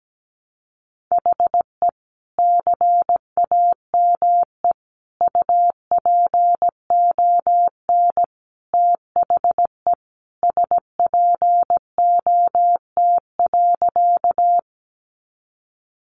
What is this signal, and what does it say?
Morse code, heard as a keyed tone, about 17 wpm, 700 Hz. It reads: HE CAME UPON THE SPOT.
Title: Sentences from Top 1100 Words 17wpm